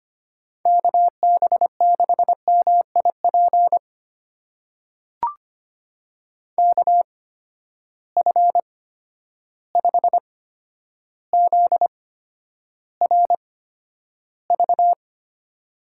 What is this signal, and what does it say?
Morse code, heard as a keyed tone, about 25 wpm, spaced as for 11 wpm, 700 Hz. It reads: KB6MIP KF5ZRV
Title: USA Call Signs - 8x Character Spacing - 25wpm